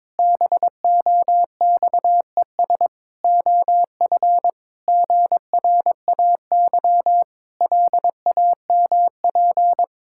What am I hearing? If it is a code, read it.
Morse code, heard as a keyed tone, about 22 wpm, 700 Hz. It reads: BOXES OF GRAY LAMP
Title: Pangram Challenge v2 - 22wpm